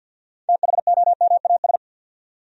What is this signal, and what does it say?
Morse code, heard as a keyed tone, about 51 wpm, 700 Hz. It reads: THOMAS
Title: USA Common Names - 50wpm